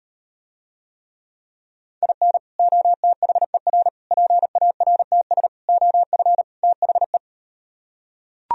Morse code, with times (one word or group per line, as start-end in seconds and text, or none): 2.0-2.4 IN
2.6-3.9 OTHER
4.1-5.5 PARTS
5.7-6.4 OF
6.6-7.2 THE